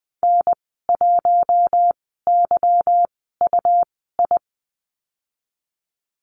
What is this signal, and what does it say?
Morse code, heard as a keyed tone, about 20 wpm, spaced as for 14 wpm, 700 Hz. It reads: N1YUI
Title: 147 Lesson 24g - Letter y - Call Signs 20wpm